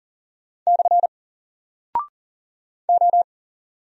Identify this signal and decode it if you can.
Morse code, heard as a keyed tone, about 40 wpm, 700 Hz. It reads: / O
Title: Single Character (Letter:Number:Punctuation:Prosign) 40wpm